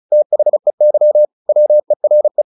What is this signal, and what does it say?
Morse code, heard as a keyed tone, about 34 wpm, 600 Hz. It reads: THEY WERE
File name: the old man and the sea / the old man and the sea 1 35wpm